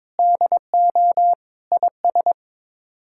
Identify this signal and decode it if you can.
Morse code, heard as a keyed tone, about 22 wpm, 700 Hz. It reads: DO IS
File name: practice / 22wpm / Sets of 7 Words - 22wpm